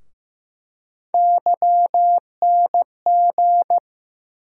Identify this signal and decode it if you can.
Morse code, heard as a keyed tone, about 15 wpm, 700 Hz. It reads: YNG
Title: USA Call Sign Postfixes - 3 Letters 15wpm